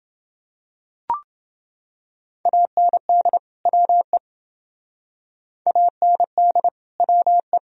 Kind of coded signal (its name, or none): Morse code